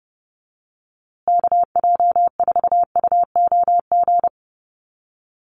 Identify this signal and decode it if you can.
Morse code, heard as a keyed tone, about 30 wpm, 700 Hz. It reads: KJ4UOG
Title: USA Call Signs Repeated 3x 30wpm